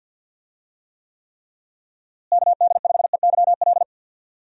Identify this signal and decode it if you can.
Morse code, heard as a keyed tone, about 52 wpm, 700 Hz. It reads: KD5EYL
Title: USA Call Signs - 6 Characters - Repeated 3x - 50wpm